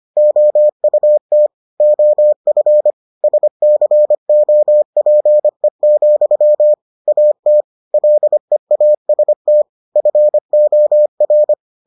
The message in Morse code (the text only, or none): OUT OF SCOPE, AT LEAST FOR